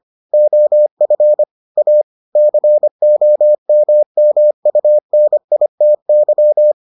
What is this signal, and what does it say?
Morse code, heard as a keyed tone, about 25 wpm, 600 Hz. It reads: OF A COMMUNITY